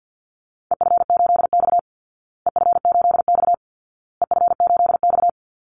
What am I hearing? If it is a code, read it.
Morse code, heard as a keyed tone, about 50 wpm, 700 Hz. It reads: EF8X EF8X EF8X
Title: Call Signs Repeated 2x - 50wpm